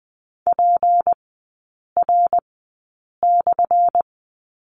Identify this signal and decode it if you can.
Morse code, heard as a keyed tone, about 20 wpm, spaced as for 8 wpm, 700 Hz. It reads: PR/
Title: 178 Lesson 29e - Slash - Three Characters 20wpm